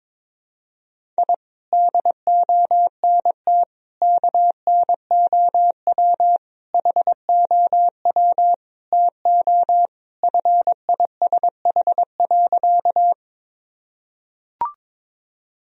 Morse code, heard as a keyed tone, about 22 wpm, 700 Hz. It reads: I DONT KNOW HOW TO FISH.
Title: Sentences from Top 1000 Words - Review Part 1 - 22wpm